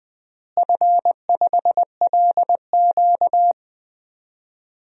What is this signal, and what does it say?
Morse code, heard as a keyed tone, about 20 wpm, 700 Hz. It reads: F5LQ
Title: DXCC-100 Call Signs - 4 Characters - Repeated 3x - 20wpm